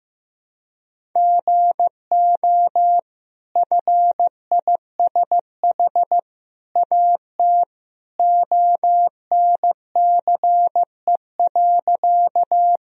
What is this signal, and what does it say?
Morse code, heard as a keyed tone, about 15 wpm, 700 Hz. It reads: GO FISH AT ONCE.
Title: Sentences from Top 1000 Words - Review Part 1 - 15wpm